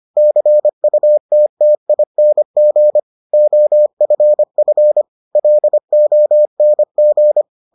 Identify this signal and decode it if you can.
Morse code, heard as a keyed tone, about 25 wpm, 600 Hz. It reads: CUTTING OFF LONG